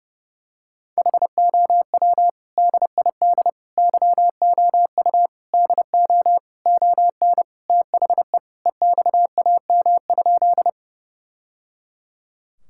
Morse code, heard as a keyed tone, about 32 wpm, 700 Hz. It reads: HOW DID YOU DO ON THE EXAM?
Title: Sentences from Top 1000 Words - Review Part 5 - 30wpm